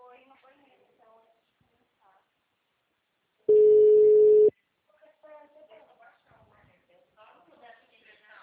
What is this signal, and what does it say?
Ringback tone (Europe, 425 Hz), with speech.
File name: out- -2028- - - .22668